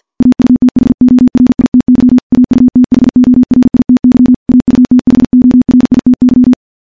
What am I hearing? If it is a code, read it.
Morse code, heard as a keyed tone, about 52 wpm, 250 Hz. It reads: AUTHORITY AUTHORITY AUTHORITY
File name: morse / morse-training-50WPM